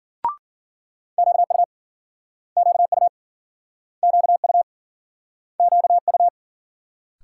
Morse code, heard as a keyed tone, about 53 wpm, 700 Hz. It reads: QU QU QU QU